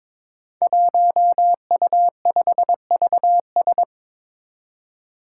Morse code, heard as a keyed tone, about 22 wpm, 700 Hz. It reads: 1U5VS
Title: Five Challenging Letter-Number Combinations 22wpm